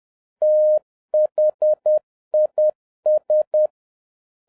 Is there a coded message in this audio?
Yes: Morse code from 0.4 to 3.7 s, heard as a keyed tone, about 10 wpm, 610 Hz. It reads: THIS